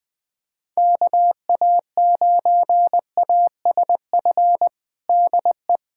Morse code, heard as a keyed tone, about 20 wpm, 700 Hz. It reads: KA9ASF DE
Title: USA Callsign DE Callsign 20wpm